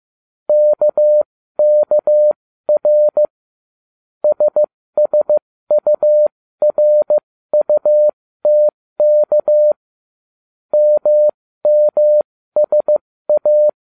Morse code, heard as a keyed tone, about 15 wpm, spaced as for 13 wpm, 600 Hz. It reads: KKR SSURUTK MMSA
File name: cw / ubacw07 1